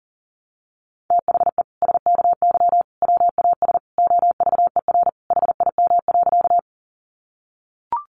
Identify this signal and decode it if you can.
Morse code, heard as a keyed tone, about 40 wpm, 700 Hz. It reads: THE SKY WAS OVER HIM.
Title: Sentences from Top 1200 Words 40wpm